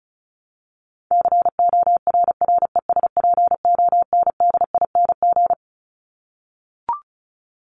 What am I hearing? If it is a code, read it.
Morse code, heard as a keyed tone, about 34 wpm, 700 Hz. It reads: CORRESPONDING
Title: ING Suffix - Top 100 Words with Base Words 35wpm